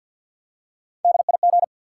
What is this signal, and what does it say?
Morse code, heard as a keyed tone, about 48 wpm, 700 Hz. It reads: DIG
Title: Three Letter Words - 50wpm